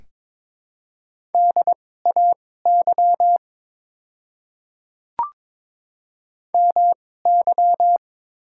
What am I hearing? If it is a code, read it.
Morse code, heard as a keyed tone, about 22 wpm, spaced as for 16 wpm, 700 Hz. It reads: DAY MY
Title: 2x Character Spacing - Top 100 Words - 22wpm